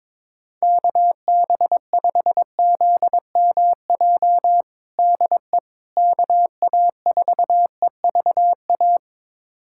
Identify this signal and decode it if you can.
Morse code, heard as a keyed tone, about 22 wpm, 700 Hz. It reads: KB5ZMJ DE KA4EVA
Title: USA Callsign DE Callsign - 6 Characters - 22wpm